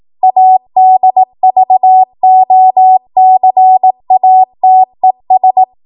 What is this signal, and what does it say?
Morse code, heard as a keyed tone, about 18 wpm, 750 Hz. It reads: ADVOCATES
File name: Morse / 18WPM